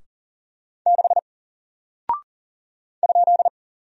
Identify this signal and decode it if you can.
Morse code, heard as a keyed tone, about 38 wpm, 700 Hz. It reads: 6 ?
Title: Single Character (Letter:Number:Punctuation:Prosign) 40wpm